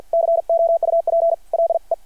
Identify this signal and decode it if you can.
Morse code, heard as a keyed tone, about 52 wpm, spaced as for 43 wpm, 660 Hz. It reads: COUPLE
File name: archived / CRSnet